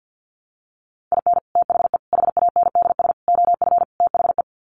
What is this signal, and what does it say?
Morse code, heard as a keyed tone, about 50 wpm, 700 Hz. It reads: IN THE HANDS OF THE
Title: Sets of 5 Words 50wpm